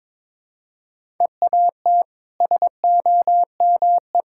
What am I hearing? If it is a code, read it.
Morse code, heard as a keyed tone, about 22 wpm, 700 Hz. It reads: EAT SOME
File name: practice / 22wpm / Easy Sentences with Hints - 22wpm